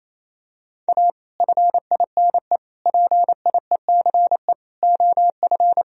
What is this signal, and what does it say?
Morse code, heard as a keyed tone, about 28 wpm, 700 Hz. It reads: A FINE PIECE OF
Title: Sentences from Top 1000 Words - Review Part 2 - 28wpm